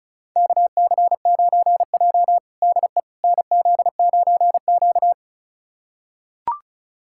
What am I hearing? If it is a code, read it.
Morse code, heard as a keyed tone, about 36 wpm, 700 Hz. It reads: KC9J DE NZ9Q
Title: USA Callsign DE Callsign - 4 Characters - 35wpm